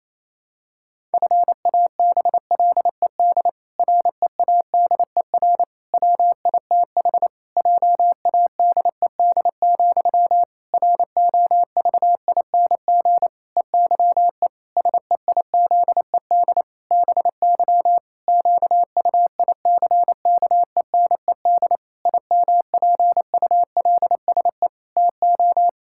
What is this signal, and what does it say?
Morse code, heard as a keyed tone, about 28 wpm, 700 Hz. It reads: FABLED READER WITH JADED, ROVING EYE SEIZED BY QUICKENED IMPULSE TO
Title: Pangram Challenge v2 - 28wpm